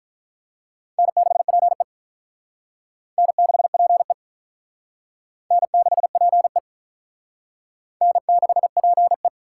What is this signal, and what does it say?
Morse code, heard as a keyed tone, about 43 wpm, 700 Hz. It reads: N6PE N6PE N6PE N6PE
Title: Speed Racing - USA Call Signs 35wpm